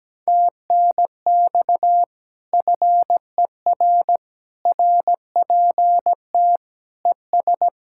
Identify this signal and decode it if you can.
Morse code, heard as a keyed tone, about 17 wpm, 700 Hz. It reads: TNX FER RPT ES
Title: Speed-Racing Common QSO Phrases - 17wpm